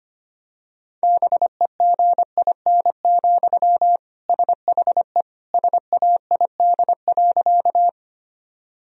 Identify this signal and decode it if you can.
Morse code, heard as a keyed tone, about 25 wpm, 700 Hz. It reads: BEGIN, SHE SAID.
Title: Sentences from Top 1000 Words - Review Part 1 - 25wpm